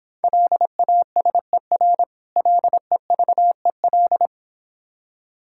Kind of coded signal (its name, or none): Morse code